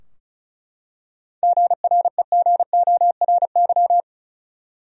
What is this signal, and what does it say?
Morse code, heard as a keyed tone, about 36 wpm, 700 Hz. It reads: GREGORY
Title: USA Common Names - 35wpm